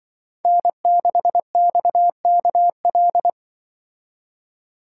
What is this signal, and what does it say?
Morse code, heard as a keyed tone, about 25 wpm, 700 Hz. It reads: N6XKL
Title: Speed Racing - USA 1x3 Call Signs 20wpm